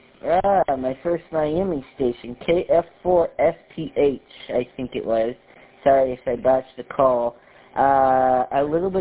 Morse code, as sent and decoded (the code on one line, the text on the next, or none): none